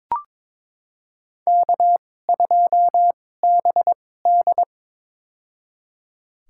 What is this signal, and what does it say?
Morse code, heard as a keyed tone, about 22 wpm, spaced as for 16 wpm, 700 Hz. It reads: K2BD
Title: USA Call Signs - 2x Character Spacing - 22wpm